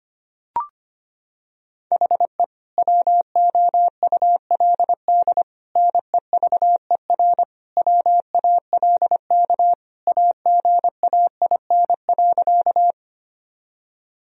Morse code, heard as a keyed tone, about 25 wpm, 700 Hz. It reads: HE WOULD NEVER WALK AGAIN.
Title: Sentences from Top 1000 Words - Review Part 1 - 25wpm